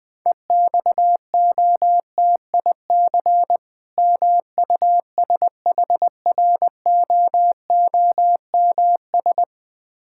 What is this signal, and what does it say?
Morse code, heard as a keyed tone, about 20 wpm, 700 Hz. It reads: EXOTIC MUSHROOMS
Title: Difficult Words with Hints - 20wpm